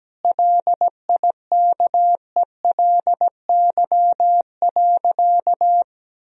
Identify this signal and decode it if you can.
Morse code, heard as a keyed tone, about 17 wpm, 700 Hz. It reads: LIKELY.
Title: Sentences from Top 1000 Words - Review Part 4 - 17wpm